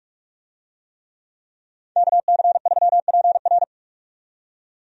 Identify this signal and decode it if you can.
Morse code, heard as a keyed tone, about 44 wpm, 700 Hz. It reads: KC3PR
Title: USA Call Signs - 5 Characters - 45wpm